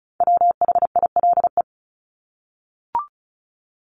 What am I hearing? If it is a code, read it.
Morse code, heard as a keyed tone, about 34 wpm, 700 Hz. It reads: WHILE
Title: Top 300 Words Repeated 35wpm